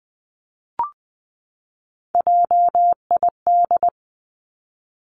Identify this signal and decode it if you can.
Morse code, heard as a keyed tone, about 20 wpm, 700 Hz. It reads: JID
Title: Three Challenging Letter-Number Combinations 20wpm